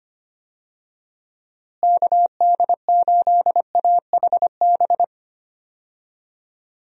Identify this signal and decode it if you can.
Morse code, heard as a keyed tone, about 25 wpm, 700 Hz. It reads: KD8AHB
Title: USA Call Signs - 6 Characters - Repeated 2x - 25wpm